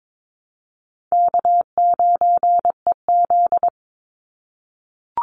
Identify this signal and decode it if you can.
Morse code, heard as a keyed tone, about 22 wpm, 700 Hz. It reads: K9EZ
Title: NA SOTA Chaser Callsigns 22wpm